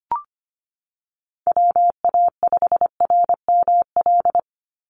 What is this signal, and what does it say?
Morse code, heard as a keyed tone, about 25 wpm, 700 Hz. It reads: WA5RML
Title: Call Signs - 25wpm